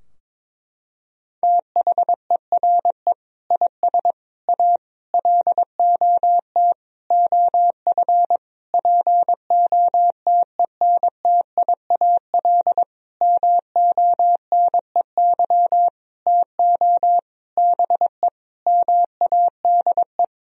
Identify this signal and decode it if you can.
Morse code, heard as a keyed tone, about 22 wpm, 700 Hz. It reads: THERE IS A LOT OF POTENTIAL MONEY TO BE MADE
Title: Sentences from Top 1000 Words - Review Part 4 - 22wpm